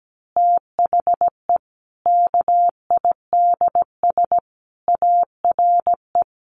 Morse code, heard as a keyed tone, about 17 wpm, 700 Hz. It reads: THE KIDS ARE
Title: Easy Phrases 17wpm